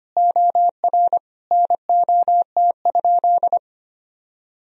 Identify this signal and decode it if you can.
Morse code, heard as a keyed tone, about 25 wpm, 700 Hz. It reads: OR NOT?
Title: Sentences from Top 1000 Words - Review Part 3 - 25wpm